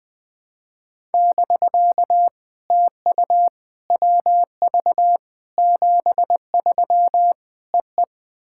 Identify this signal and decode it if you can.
Morse code, heard as a keyed tone, about 21 wpm, 700 Hz. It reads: <BK> TU WV 73 EE
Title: POTA QSO Practice - 20wpm